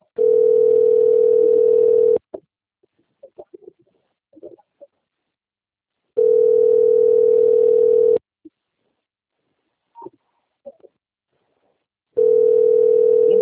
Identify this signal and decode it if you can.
Ringback tone (North America), 440 + 480 Hz.
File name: rg-1600-unknown- - - .70311